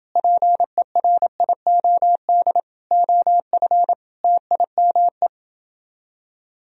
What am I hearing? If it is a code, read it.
Morse code, heard as a keyed tone, about 30 wpm, 700 Hz. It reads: PERIOD OF TIME